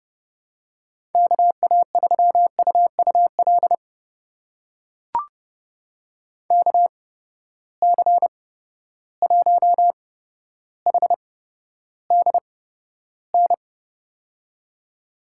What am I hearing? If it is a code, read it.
Morse code, heard as a keyed tone, about 30 wpm, spaced as for 13 wpm, 700 Hz. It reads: KA3UUL KC1HDN
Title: USA Call Signs - 8x Character Spacing - 30wpm